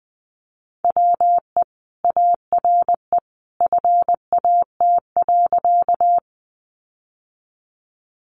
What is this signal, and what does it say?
Morse code, heard as a keyed tone, about 20 wpm, 700 Hz. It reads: WE ARE FAT.
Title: Sentences from Top 600 Words 20wpm